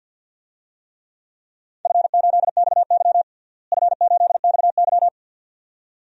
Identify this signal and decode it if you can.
Morse code, heard as a keyed tone, about 52 wpm, 700 Hz. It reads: F8XY F8XY
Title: DXCC-100 Call Signs - 4 Characters - Repeated 2x - 50wpm